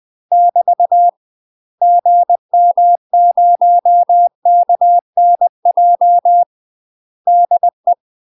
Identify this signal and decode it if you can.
Morse code, heard as a keyed tone, about 20 wpm, 700 Hz. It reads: <BT> GM0KNJ DE
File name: UK MASTER QSOs TRAINING / 20WPM / UK MASTER QSOs 0024